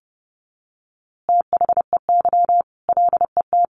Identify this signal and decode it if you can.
Morse code, heard as a keyed tone, about 30 wpm, 700 Hz. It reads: THEY LET
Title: A Sound Of Thunder 30wpm